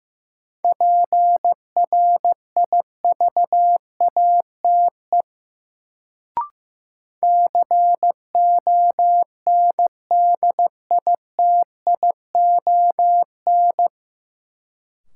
Morse code, heard as a keyed tone, about 15 wpm, 700 Hz. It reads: PRIVATE CONDITION
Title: Sentences from Top 1000 Words - New Word Drill - 15wpm